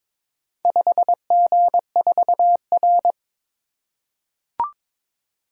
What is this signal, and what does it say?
Morse code, heard as a keyed tone, about 22 wpm, 690 Hz. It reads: 5G4R
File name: practice / 22wpm / Four Letter-Number Combinations - 22wpm